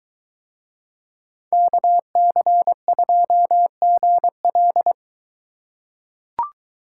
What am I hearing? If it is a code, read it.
Morse code, heard as a keyed tone, about 22 wpm, 700 Hz. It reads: KC2GL